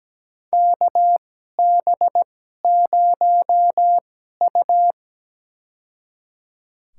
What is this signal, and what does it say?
Morse code, heard as a keyed tone, about 17 wpm, spaced as for 12 wpm, 700 Hz. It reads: KB0U
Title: USA Call Signs - 2x Character Spacing - 17wpm